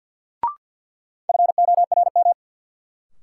Morse code, heard as a keyed tone, about 48 wpm, 700 Hz. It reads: FORM
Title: Sets of 4 Words - New Word Drill - 50wpm